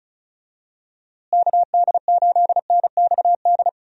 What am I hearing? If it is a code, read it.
Morse code, heard as a keyed tone, about 36 wpm, 700 Hz. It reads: KD8NXD
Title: USA Call Signs - 6 Characters - Repeated 3x - 35wpm